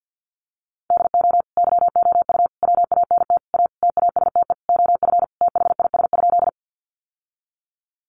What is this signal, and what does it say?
Morse code, heard as a keyed tone, about 50 wpm, 700 Hz. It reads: DO YOU WANT A TASTE OF THIS?
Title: Sentences from Top 1100 Words 50wpm